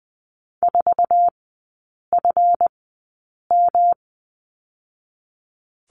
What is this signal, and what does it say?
Morse code, heard as a keyed tone, about 20 wpm, spaced as for 8 wpm, 700 Hz. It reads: 4FM
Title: 138 Lesson 23e - Letter f - Three Characters 20wpm